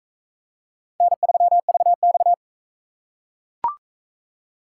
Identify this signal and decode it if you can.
Morse code, heard as a keyed tone, about 40 wpm, 700 Hz. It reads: N3VX